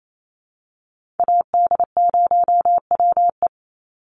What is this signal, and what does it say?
Morse code, heard as a keyed tone, about 28 wpm, 700 Hz. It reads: AD0WE
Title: Call Signs Repeated 2x - 28wpm